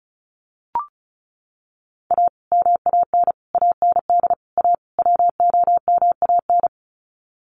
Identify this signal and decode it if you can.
Morse code, heard as a keyed tone, about 34 wpm, 700 Hz. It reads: A MAN AND A WOMAN
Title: Sets of 5 Words Encore 35wpm